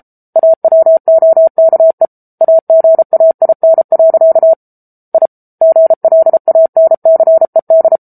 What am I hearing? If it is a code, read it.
Morse code, heard as a keyed tone, about 34 wpm, 650 Hz. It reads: AWOKE AGAIN. I GLANCED
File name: Book Two / 07-Chapter 7 - 33 WPM